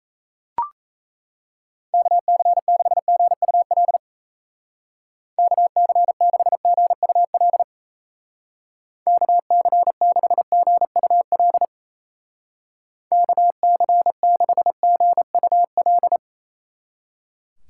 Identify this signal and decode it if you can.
Morse code, heard as a keyed tone, about 34 wpm, 700 Hz. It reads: KC6GUL KC6GUL KC6GUL KC6GUL
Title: Speed Racing - USA Call Signs 28wpm